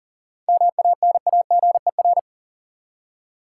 Morse code, heard as a keyed tone, about 40 wpm, 700 Hz. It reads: MANAGER
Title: Seven Letter Words - 40wpm